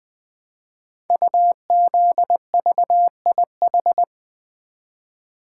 Morse code, heard as a keyed tone, about 20 wpm, 700 Hz. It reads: UZVIH
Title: Code Groups - 5 Letters - 20wpm